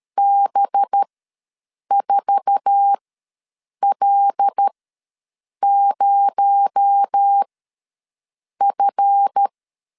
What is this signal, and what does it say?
Morse code, heard as a keyed tone, about 13 wpm, spaced as for 7 wpm, 800 Hz. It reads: B4L0F
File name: LEZIONI / 8 YJ45 / 8 YJ4 ws 12wpm 3249